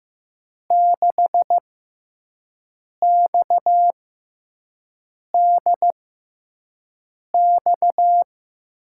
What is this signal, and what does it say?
Morse code, heard as a keyed tone, about 15 wpm, spaced as for 5 wpm, 700 Hz. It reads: 6XDX
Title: USA Call Signs - 5 Characters - 6x Character Spacing - 15wpm